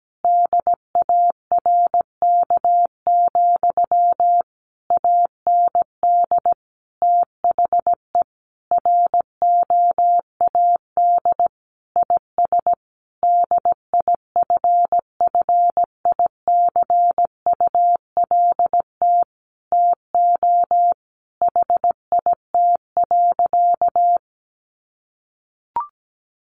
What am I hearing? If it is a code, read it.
Morse code, heard as a keyed tone, about 17 wpm, 700 Hz. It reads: DARK, AND THE ROAD IS DIFFICULT TO HIT.
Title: Sentences from Top 1200 Words 17wpm